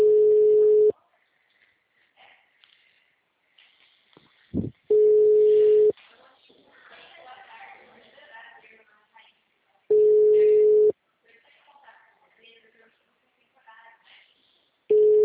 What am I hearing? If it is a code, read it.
Ringback tone (Europe, 425 Hz), with speech.
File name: out- -2002- - - .13